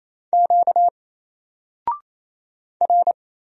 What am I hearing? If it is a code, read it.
Morse code, heard as a keyed tone, about 27 wpm, 700 Hz. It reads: Q R
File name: Single-Letter-Number-28wpm